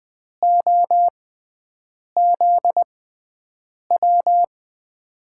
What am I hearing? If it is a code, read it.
Morse code, heard as a keyed tone, about 20 wpm, spaced as for 7 wpm, 700 Hz. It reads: OZW
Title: USA Call Signs - 6x Character Spacing - 20wpm